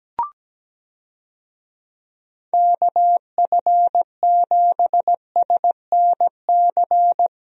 Morse code, heard as a keyed tone, about 17 wpm, 700 Hz. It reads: KF7SNC
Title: USA Call Signs - 6 Characters - Repeated 2x - 17wpm